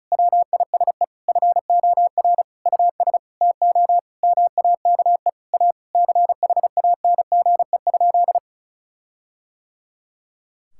Morse code, heard as a keyed tone, about 36 wpm, 700 Hz. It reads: WISE FOR US TO MAKE A CHANGE?
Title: Sentences from Top 1600 Words 35wpm